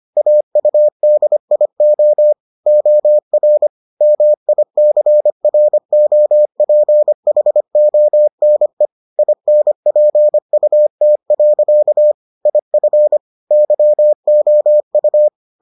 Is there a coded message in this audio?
Yes: Morse code from 0.2 to 15.3 s, heard as a keyed tone, about 25 wpm, 600 Hz. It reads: AUDIO OR MICROPHONE INPUT. IF YOU